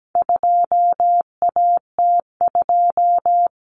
Morse code, heard as a keyed tone, about 17 wpm, 700 Hz. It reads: 2AT2
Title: Common Cut Numbers - 4-Digit 17wpm